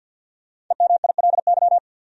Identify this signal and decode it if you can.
Morse code, heard as a keyed tone, about 51 wpm, 700 Hz. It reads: EMILY
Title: USA Common Names - Max 5 Letters - 50wpm